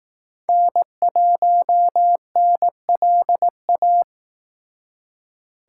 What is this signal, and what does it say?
Morse code, heard as a keyed tone, about 18 wpm, 700 Hz. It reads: N1NLA
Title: Speed Racing - USA 1x3 Call Signs 15wpm